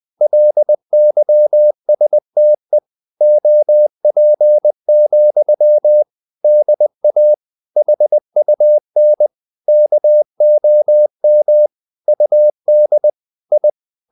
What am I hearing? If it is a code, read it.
Morse code, heard as a keyed tone, about 20 wpm, 600 Hz. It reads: LYSTE OP, DA HUN KOM UD I
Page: CW med Gnister